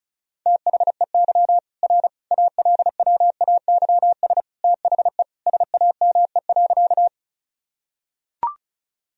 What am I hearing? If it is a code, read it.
Morse code, heard as a keyed tone, about 36 wpm, 700 Hz. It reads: THEY R ALWAYS THE SAME.
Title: Sentences from Top 300 Words and Abbreviations 35wpm